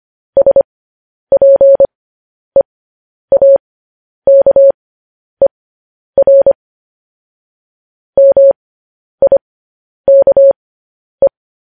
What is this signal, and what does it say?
Morse code, heard as a keyed tone, about 25 wpm, spaced as for 10 wpm, 550 Hz. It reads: SPEAKER MIKE
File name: podcast / News headlines-1x-25x10